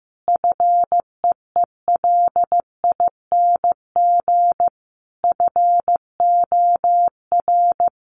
Morse code, heard as a keyed tone, about 15 wpm, 700 Hz. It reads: FEELING FOR
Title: Sentences from Top 1200 Words 15wpm